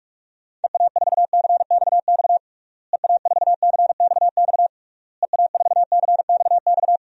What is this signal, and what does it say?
Morse code, heard as a keyed tone, about 46 wpm, 700 Hz. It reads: EA4CXX EA4CXX EA4CXX
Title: DXCC-100 Call Signs - 6 Characters - Repeated 3x - 45wpm